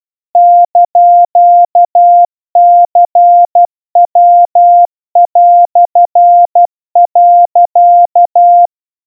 Morse code (Japanese Wabun code), heard as a keyed tone, about 12 wpm, 700 Hz. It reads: <KN>ニヤ)、
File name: kawa / Morse sample / iroha 40 cpm 700 Hz Crispy